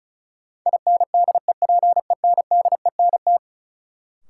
Morse code, heard as a keyed tone, about 36 wpm, 700 Hz. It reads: INDEPENDENT
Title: Sentences from Top 900 Words - New Word Drill - 35wpm